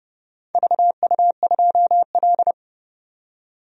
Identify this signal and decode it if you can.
Morse code, heard as a keyed tone, about 32 wpm, 700 Hz. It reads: VU2L
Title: DXCC-100 Call Signs - 4 Characters - 30wpm